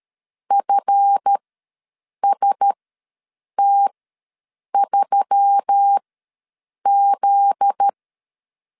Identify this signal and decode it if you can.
Morse code, heard as a keyed tone, about 13 wpm, spaced as for 7 wpm, 800 Hz. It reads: FST3Z
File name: LEZIONI / 7 GF9Z / 7 GF9Z ws 12wpm 3321